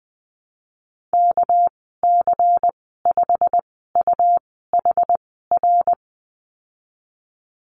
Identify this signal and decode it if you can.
Morse code, heard as a keyed tone, about 20 wpm, spaced as for 15 wpm, 700 Hz. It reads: KC5UHR
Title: 238 Lesson 38g - Letter q - Call Signs 20wpm